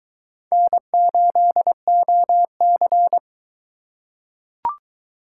Morse code, heard as a keyed tone, about 21 wpm, 700 Hz. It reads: N8OC